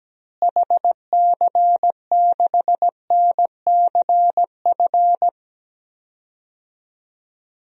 Morse code, heard as a keyed tone, about 17 wpm, 700 Hz. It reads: HC6NCF
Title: DXCC-100 Call Signs - 6 Characters - Repeated 3x - 17wpm